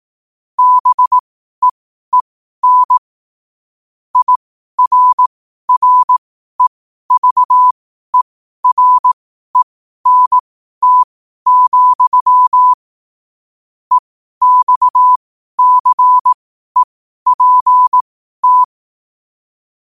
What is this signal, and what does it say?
Morse code, heard as a keyed tone, about 18 wpm, spaced as for 12 wpm, 1000 Hz. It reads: BEEN IRREVERENT, EXCEPT
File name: qotdfiles / 12wpm